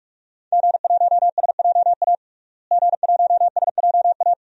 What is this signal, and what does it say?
Morse code, heard as a keyed tone, about 46 wpm, 700 Hz. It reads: G1SJA G1SJA
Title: DXCC-100 Call Signs - 5 Characters - Repeated 2x - 45wpm